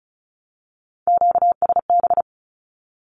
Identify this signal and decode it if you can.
Morse code, heard as a keyed tone, about 34 wpm, 700 Hz. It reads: QSB
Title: Q-Codes 35wpm